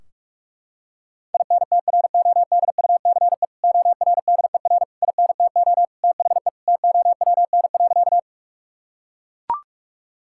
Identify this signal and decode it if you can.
Morse code, heard as a keyed tone, about 41 wpm, 700 Hz. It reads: INTRODUCE ORDER INTO THE TOWN.
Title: Sentences from Top 1600 Words 45wpm